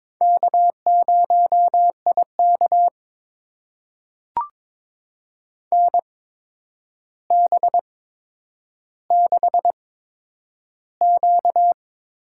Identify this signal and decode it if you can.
Morse code, heard as a keyed tone, about 22 wpm, spaced as for 9 wpm, 700 Hz. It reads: K0IK NB6Q
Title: USA Call Signs - 4 Characters - 8x Character Spacing - 22wpm